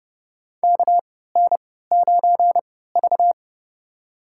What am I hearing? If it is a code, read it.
Morse code, heard as a keyed tone, about 28 wpm, spaced as for 17 wpm, 700 Hz. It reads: KN9V
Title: USA Call Signs - 4 Characters - 3x Character Spacing - 30wpm